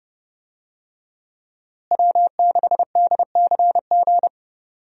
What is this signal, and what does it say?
Morse code, heard as a keyed tone, about 31 wpm, 700 Hz. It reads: W6DCG
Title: USA Call Signs - 5 Characters - Repeated 2x - 30wpm